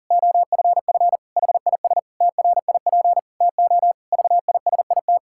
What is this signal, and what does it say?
Morse code, heard as a keyed tone, about 40 wpm, 700 Hz. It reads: OFF HIS TRIP TO VISIT
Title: Pangram Challenge v2 - 40wpm